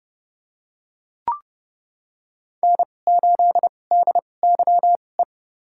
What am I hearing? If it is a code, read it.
Morse code, heard as a keyed tone, about 31 wpm, spaced as for 22 wpm, 700 Hz. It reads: N8DYE
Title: USA Call Signs - 5 Characters - 2x Character Spacing - 30wpm